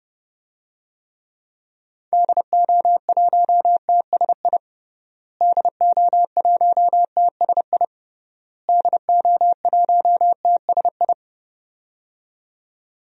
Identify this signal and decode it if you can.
Morse code, heard as a keyed tone, about 29 wpm, 700 Hz. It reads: DO1TSI DO1TSI DO1TSI
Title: DXCC-100 Call Signs - 6 Characters - Repeated 3x - 30wpm